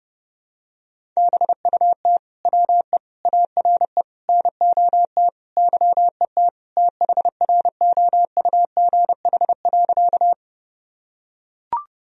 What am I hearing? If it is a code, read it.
Morse code, heard as a keyed tone, about 30 wpm, 700 Hz. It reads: BUT WE ARE NOT YET THROUGH.
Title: Sentences from Top 1000 Words - Review Part 1 - 30wpm